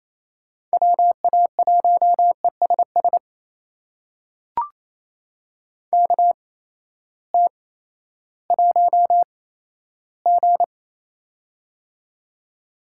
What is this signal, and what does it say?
Morse code, heard as a keyed tone, about 28 wpm, spaced as for 14 wpm, 700 Hz. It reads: WA1ESS KT1G
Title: USA Call Signs - 8x Character Spacing - 28wpm